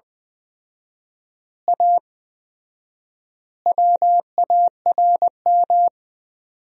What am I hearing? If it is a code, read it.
Morse code, heard as a keyed tone, about 21 wpm, 700 Hz. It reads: A WARM
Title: 4x Word-Spacing - Sentences from Top 500 Words - 20wpm